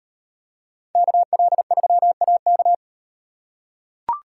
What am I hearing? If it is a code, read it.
Morse code, heard as a keyed tone, about 38 wpm, 700 Hz. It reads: KL3AK